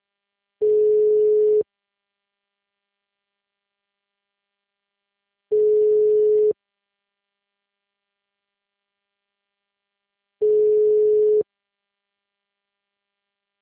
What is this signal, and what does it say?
Ringback tone (Europe, 425 Hz).